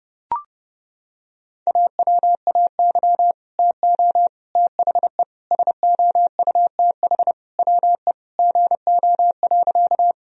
Morse code, heard as a keyed tone, about 32 wpm, 700 Hz. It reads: AWAY TO THE SOUTH WE GO.
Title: Sentences from Top 1000 Words - Review Part 2 - 30wpm